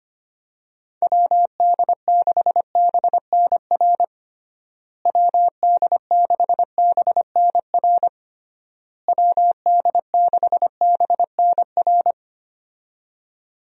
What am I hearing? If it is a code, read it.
Morse code, heard as a keyed tone, about 25 wpm, 700 Hz. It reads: WD6BNR WD6BNR WD6BNR
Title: USA Call Signs - 6 Characters - Repeated 3x - 25wpm